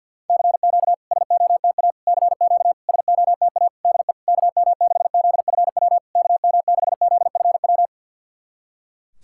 Morse code, heard as a keyed tone, about 53 wpm, 690 Hz. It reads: CQ SOTA CQ SOTA DE KM6ZFW KM6ZFW
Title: SOTA QSO Practice 50wpm